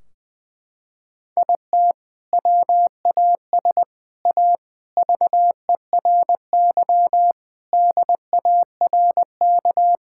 Morse code, heard as a keyed tone, about 20 wpm, 700 Hz. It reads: IT WAS A VERY DARK
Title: Sentences from Top 1600 Words 20wpm